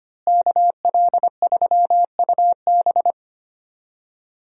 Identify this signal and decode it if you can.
Morse code, heard as a keyed tone, about 25 wpm, 700 Hz. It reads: KL3UB
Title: USA Call Signs - 5 Characters - Repeated 2x - 25wpm